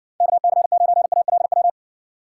Morse code, heard as a keyed tone, about 60 wpm, 700 Hz. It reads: KC9ALW